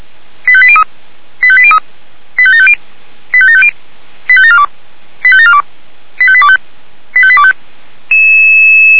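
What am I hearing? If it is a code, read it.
Five-tone selective call (ZVEI), heard as quick runs of five tones, tones of about 70 ms. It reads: 85803 85803 85750 85750 85742 85742 85825 85825
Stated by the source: ZVEI five-tone selective call 85825, received by SDR.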